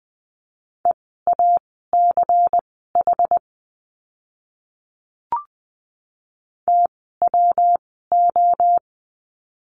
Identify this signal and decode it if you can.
Morse code, heard as a keyed tone, about 20 wpm, spaced as for 14 wpm, 700 Hz. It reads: EACH TWO
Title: 192 Lesson 31f - Letter v - Words 20wpm